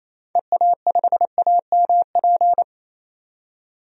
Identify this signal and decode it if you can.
Morse code, heard as a keyed tone, about 28 wpm, 700 Hz. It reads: EA5AMP
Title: DXCC-100 Call Signs - 6 Characters - 28wpm